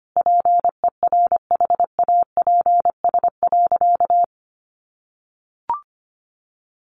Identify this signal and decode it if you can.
Morse code, heard as a keyed tone, about 25 wpm, 700 Hz. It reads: PERHAPS.
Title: Sentences from Top 1400 Words 25wpm